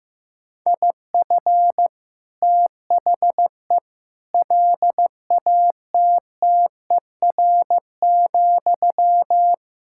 Morse code, heard as a keyed tone, about 15 wpm, 700 Hz. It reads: IF THE LATTER,
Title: Sentences from Top 1600 Words 15wpm